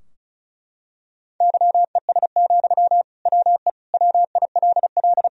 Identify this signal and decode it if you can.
Morse code, heard as a keyed tone, about 36 wpm, 700 Hz. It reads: YES, WE WILL
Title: Sentences from Top 1000 Words - Review Part 2 - 35wpm